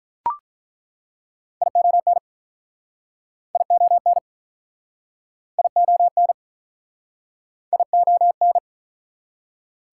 Morse code, heard as a keyed tone, about 48 wpm, 700 Hz. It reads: ION ION ION ION